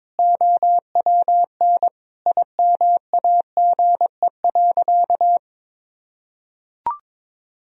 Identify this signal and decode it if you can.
Morse code, heard as a keyed tone, about 22 wpm, 700 Hz. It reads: OWN IMAGE.
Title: Sentences from Top 1000 Words - Review Part 4 - 22wpm